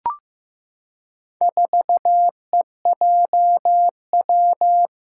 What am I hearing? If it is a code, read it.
Morse code, heard as a keyed tone, about 15 wpm, 700 Hz. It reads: 4EJW
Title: Four Challenging Letter-Number Combinations 15wpm